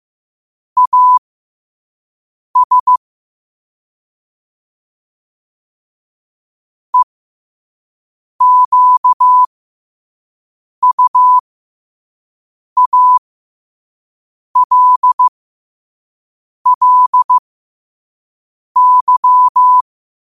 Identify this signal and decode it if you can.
Morse code, heard as a keyed tone, about 15 wpm, spaced as for 5 wpm, 1000 Hz. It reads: AS EQUALLY
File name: qotdfiles / 5wpm